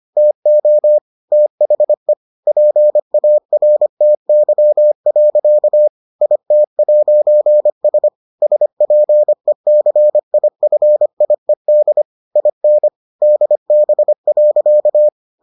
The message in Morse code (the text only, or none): TO THE PARTY. IT'S SPECIFIED IN DB.